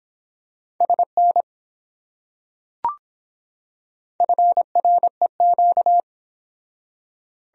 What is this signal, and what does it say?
Morse code, heard as a keyed tone, about 23 wpm, 700 Hz. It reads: SN FREQ